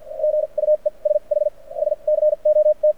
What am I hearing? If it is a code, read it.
Morse code, heard as a keyed tone, about 48 wpm, spaced as for 37 wpm, 600 Hz. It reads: GUESH SCOT
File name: archived / CRSnet